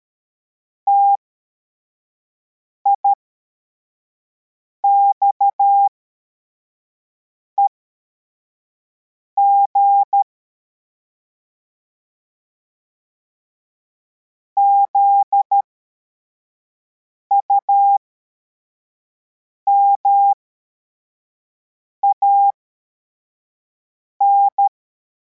Morse code, heard as a keyed tone, about 13 wpm, spaced as for 4 wpm, 800 Hz. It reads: TIXEG ZUMAN
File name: formation / cw / Text41